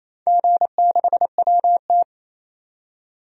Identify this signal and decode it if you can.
Morse code, heard as a keyed tone, about 28 wpm, 700 Hz. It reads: G6WT
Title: DXCC-100 Call Signs - 4 Characters - 28wpm